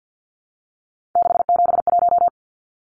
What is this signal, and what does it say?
Morse code, heard as a keyed tone, about 50 wpm, 700 Hz. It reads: 671
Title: Numbers - 3-Digit 50wpm